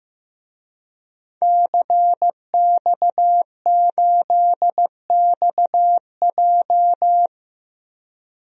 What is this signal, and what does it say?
Morse code, heard as a keyed tone, about 15 wpm, 700 Hz. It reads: CX8XJ
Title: DXCC-100 Call Signs - 5 Characters - 15wpm